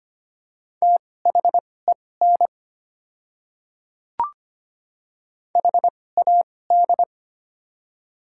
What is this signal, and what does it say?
Morse code, heard as a keyed tone, about 25 wpm, spaced as for 18 wpm, 700 Hz. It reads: THEN HAD
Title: 2x Character Spacing - Top 100 Words - 25wpm